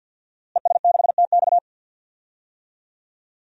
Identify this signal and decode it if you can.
Morse code, heard as a keyed tone, about 51 wpm, 700 Hz. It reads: ES6TX
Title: DXCC-100 Call Signs - 5 Characters - Repeated 2x - 50wpm